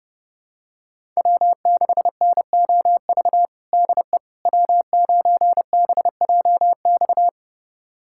Morse code, heard as a keyed tone, about 30 wpm, 700 Hz. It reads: W6NOV DE W9BJX
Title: USA Callsign DE Callsign - 5 Characters - 30wpm